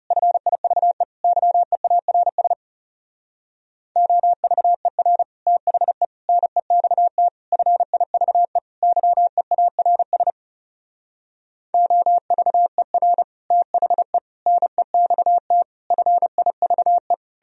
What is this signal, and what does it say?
Morse code, heard as a keyed tone, about 38 wpm, 700 Hz. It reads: FIVE YEARS OVER THE NEXT FIVE YEARS OVER THE NEXT FIVE